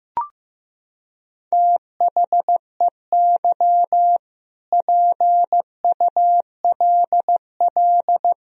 Morse code, heard as a keyed tone, about 15 wpm, 700 Hz. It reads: THEY PULL
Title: Sentences from Top 1000 Words - Review Part 2 - 15wpm